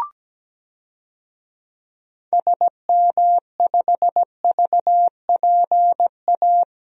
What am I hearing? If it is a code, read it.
Morse code, heard as a keyed tone, about 17 wpm, 700 Hz. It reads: SM5VPA
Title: DXCC-100 Call Signs - 6 Characters - 17wpm